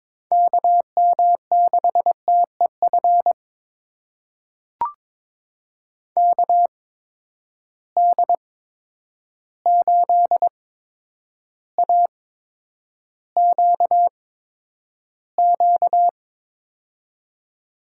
Morse code, heard as a keyed tone, about 22 wpm, spaced as for 9 wpm, 700 Hz. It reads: KM6TEF KD8AQQ
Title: USA Call Signs - 6 Characters - 8x Character Spacing - 22wpm